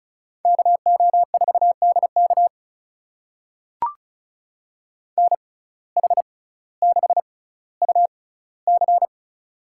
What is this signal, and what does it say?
Morse code, heard as a keyed tone, about 36 wpm, spaced as for 18 wpm, 700 Hz. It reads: KO4DK NH6UC
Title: USA Call Signs - 6x Character Spacing - 35wpm